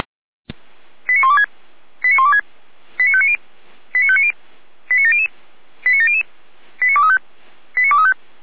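Five-tone selective call (ZVEI), heard as quick runs of five tones, tones of about 70 ms. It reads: 89126 89126 89590 89590 89700 89700 89235 89235
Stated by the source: Five-tone ZVEI selective call 89025, received by SDR.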